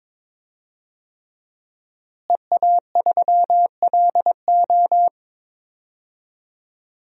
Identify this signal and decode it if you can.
Morse code, heard as a keyed tone, about 21 wpm, 700 Hz. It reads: EA3LO
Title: DXCC-100 Call Signs - 5 Characters - Repeated 3x - 22wpm